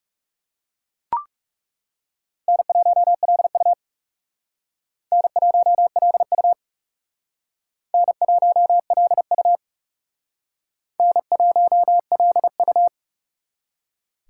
Morse code, heard as a keyed tone, about 36 wpm, 700 Hz. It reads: N1LU N1LU N1LU N1LU
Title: Speed Racing - USA 1x2 Call Signs 30wpm